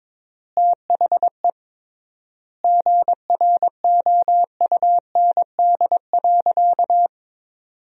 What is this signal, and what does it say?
Morse code, heard as a keyed tone, about 22 wpm, 700 Hz. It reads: THE GROUND.
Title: 3x Word-Spacing - Sentences from Top 500 Words - 22wpm